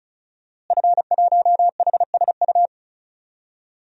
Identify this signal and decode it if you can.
Morse code, heard as a keyed tone, about 36 wpm, 700 Hz. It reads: F1HSU
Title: DXCC-100 Call Signs - 5 Characters - Repeated 2x - 35wpm